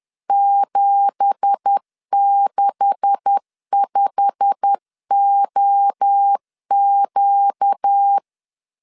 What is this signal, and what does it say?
Morse code, heard as a keyed tone, about 10 wpm, 800 Hz. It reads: 765OQ
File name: LEZIONI / 9 2867 10wpm 2733